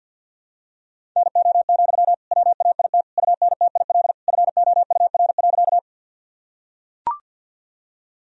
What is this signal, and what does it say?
Morse code, heard as a keyed tone, about 45 wpm, 700 Hz. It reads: NO, WAIT UNTIL FOUR.
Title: Sentences from Top 1000 Words - Review Part 1 - 50wpm